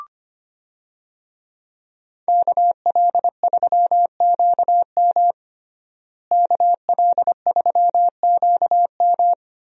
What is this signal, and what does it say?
Morse code, heard as a keyed tone, about 25 wpm, 700 Hz. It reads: KL3QM KL3QM
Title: USA Call Signs - 5 Characters - Repeated 2x - 25wpm